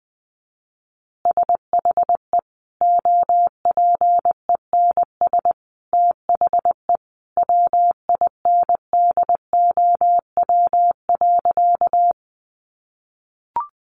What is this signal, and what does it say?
Morse code, heard as a keyed tone, about 19 wpm, 700 Hz. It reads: SHE OPENS THE WINDOW.
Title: Easy Phrases 20wpm